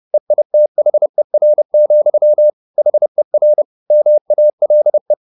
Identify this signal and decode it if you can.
Morse code, heard as a keyed tone, about 30 wpm, 600 Hz. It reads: EITHER, HER MALE